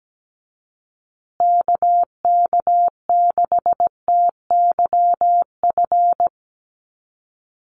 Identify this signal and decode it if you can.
Morse code, heard as a keyed tone, about 17 wpm, 700 Hz. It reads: KK6TYF
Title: USA Call Signs Repeated 2x 17wpm